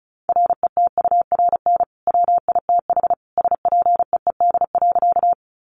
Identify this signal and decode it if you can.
Morse code, heard as a keyed tone, about 34 wpm, 700 Hz. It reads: RETURN WITH SPEED.
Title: Sentences from Top 800 Words 35wpm